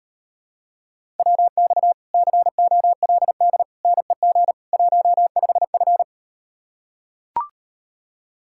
Morse code, heard as a keyed tone, about 38 wpm, 700 Hz. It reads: WX COLD NEG 15F